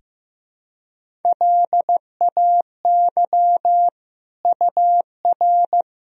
Morse code, heard as a keyed tone, about 15 wpm, 700 Hz. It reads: LAY UR
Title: Sentences from Top 400 Words and Abbreviations 15wpm